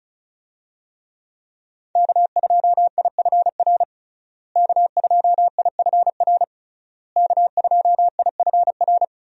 Morse code, heard as a keyed tone, about 35 wpm, 700 Hz. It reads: K2IFR K2IFR K2IFR
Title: USA Call Signs - 5 Characters - Repeated 3x - 35wpm